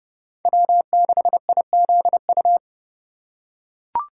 Morse code, heard as a keyed tone, about 32 wpm, 700 Hz. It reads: W6IZU
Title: USA Call Signs - 5 Characters - 4x Character Spacing - 30wpm